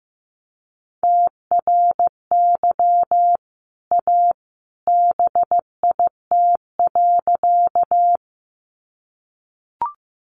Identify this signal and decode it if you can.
Morse code, heard as a keyed tone, about 15 wpm, 700 Hz. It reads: TRY A BIT.
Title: Sentences from Top 900 Words 15wpm